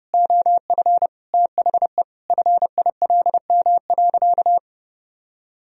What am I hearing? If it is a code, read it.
Morse code, heard as a keyed tone, about 32 wpm, 700 Hz. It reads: OF THE FILM.
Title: Sentences from Top 1000 Words - Review Part 3 - 30wpm